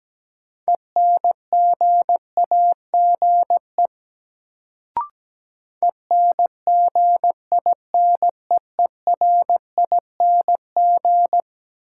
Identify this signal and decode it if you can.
Morse code, heard as a keyed tone, about 17 wpm, 700 Hz. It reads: ENGAGE ENGINEERING
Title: Sentences from Top 1500 Words - New Word Drill - 17wpm